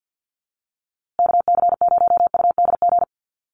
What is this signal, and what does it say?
Morse code, heard as a keyed tone, about 48 wpm, 700 Hz. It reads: KC0UDG
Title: USA Call Signs Repeated 3x 50wpm